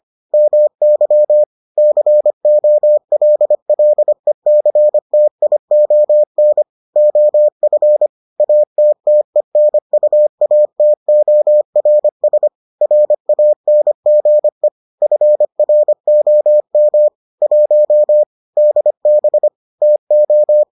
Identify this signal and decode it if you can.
Morse code, heard as a keyed tone, about 25 wpm, 600 Hz. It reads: MY COLLECTION OF ATTENUATORS RANGE FROM 1 DB TO